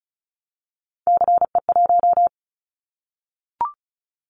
Morse code, heard as a keyed tone, about 34 wpm, 700 Hz. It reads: CE1
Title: Callsign Prefixes for Top 100 DXCC - 35wpm